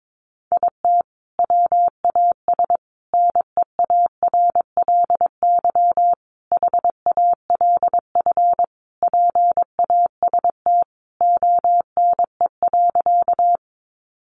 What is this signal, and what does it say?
Morse code, heard as a keyed tone, about 22 wpm, 700 Hz. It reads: IT WAS NEARLY HALF PAST ONE.
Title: Sentences from Top 1300 Words 22wpm